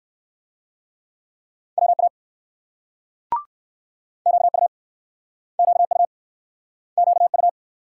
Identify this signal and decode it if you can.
Morse code, heard as a keyed tone, about 60 wpm, 700 Hz. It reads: WA QU QU QU